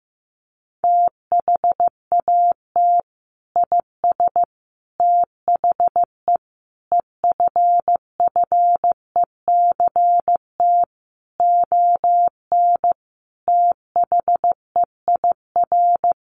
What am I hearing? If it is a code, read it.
Morse code, heard as a keyed tone, about 15 wpm, 700 Hz. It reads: THAT IS THE EFFECT ON THEIR
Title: Sentences from Top 1300 Words 15wpm